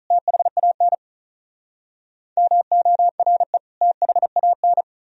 Morse code, heard as a keyed tone, about 46 wpm, 700 Hz. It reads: THAN MORE THAN